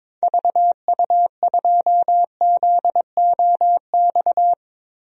Morse code, heard as a keyed tone, about 22 wpm, 700 Hz. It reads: VU2ZOX
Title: DXCC-100 Call Signs - 6 Characters - Repeated 2x - 22wpm